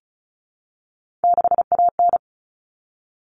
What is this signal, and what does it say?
Morse code, heard as a keyed tone, about 34 wpm, 700 Hz. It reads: 6AN
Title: All Cut Numbers - 3-Digit 35wpm